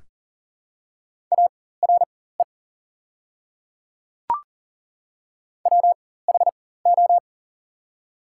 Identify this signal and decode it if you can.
Morse code, heard as a keyed tone, about 40 wpm, spaced as for 19 wpm, 700 Hz. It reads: ARE WHO
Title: 4x Character Spacing - Top 100 Words - 40wpm